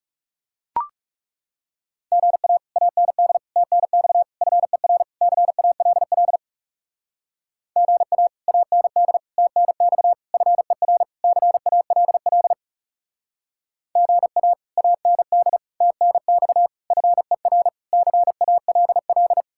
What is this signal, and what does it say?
Morse code, heard as a keyed tone, about 39 wpm, 700 Hz. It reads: GA AND TNX FER CALL GA AND TNX FER CALL GA AND TNX FER CALL